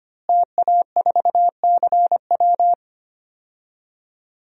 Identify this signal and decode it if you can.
Morse code, heard as a keyed tone, about 25 wpm, 700 Hz. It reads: TA4CW
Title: DXCC-100 Call Signs - 5 Characters - Repeated 2x - 25wpm